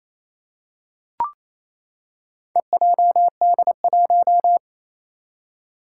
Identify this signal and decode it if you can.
Morse code, heard as a keyed tone, about 28 wpm, 700 Hz. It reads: EJD1
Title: Four Challenging Letter-Number Combinations 28wpm